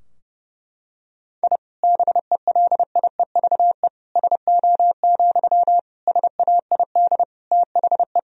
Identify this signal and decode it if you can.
Morse code, heard as a keyed tone, about 30 wpm, 700 Hz. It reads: I BELIEVE SO, SAID THE
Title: Sentences from Top 1000 Words - Review Part 4 - 30wpm